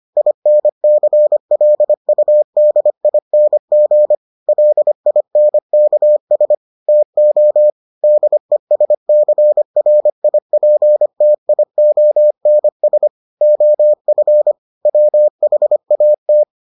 Morse code, heard as a keyed tone, about 25 wpm, 600 Hz. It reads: INCLUDING LINKS TO DESCRIPTIONS OF WHAT